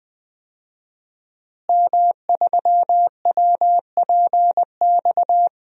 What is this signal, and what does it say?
Morse code, heard as a keyed tone, about 20 wpm, 700 Hz. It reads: M3WPX
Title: DXCC-100 Call Signs - 5 Characters - Repeated 3x - 20wpm